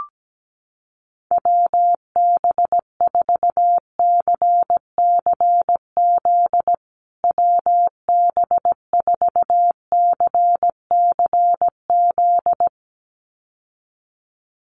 Morse code, heard as a keyed tone, about 17 wpm, 700 Hz. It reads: WB4CCZ WB4CCZ
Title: USA Call Signs Repeated 2x 17wpm